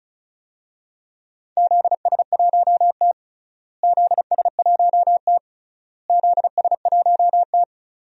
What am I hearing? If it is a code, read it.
Morse code, heard as a keyed tone, about 34 wpm, 700 Hz. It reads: ZS1T ZS1T ZS1T
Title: DXCC-100 Call Signs - 4 Characters - Repeated 3x - 35wpm